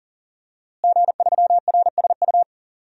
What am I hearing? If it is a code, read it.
Morse code, heard as a keyed tone, about 39 wpm, 700 Hz. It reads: G3RSU
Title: DXCC-100 Call Signs - 5 Characters - Repeated 2x - 40wpm